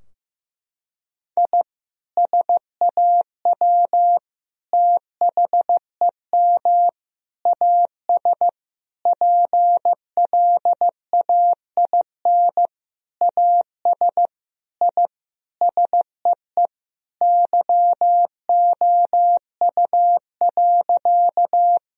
Morse code, heard as a keyed tone, about 15 wpm, 700 Hz. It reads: I SAW THEM AS PLAIN AS I SEE YOU.
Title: Sentences from Top 1000 Words - Review Part 2 - 15wpm